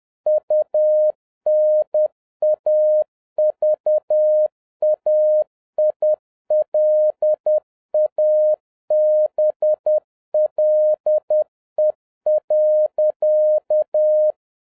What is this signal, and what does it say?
Morse code, heard as a keyed tone, about 10 wpm, 610 Hz. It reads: UNAVAILABLE.